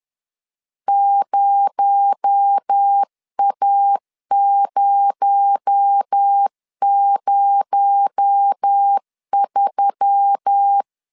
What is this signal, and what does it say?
Morse code, heard as a keyed tone, about 10 wpm, 800 Hz. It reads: 0A003
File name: LEZIONI / 6 PKX0 / 6 PKX0 10wpm 2707